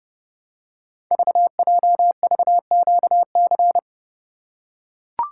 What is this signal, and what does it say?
Morse code, heard as a keyed tone, about 30 wpm, 700 Hz. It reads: VJVQC
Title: Code Groups - 5 Letters - 30wpm